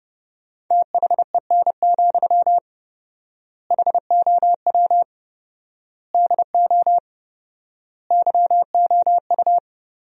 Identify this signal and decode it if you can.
Morse code, heard as a keyed tone, about 30 wpm, 700 Hz. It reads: THEN, HOW DO YOU
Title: 4x Word-Spacing - Sentences from Top 100 Words - 30wpm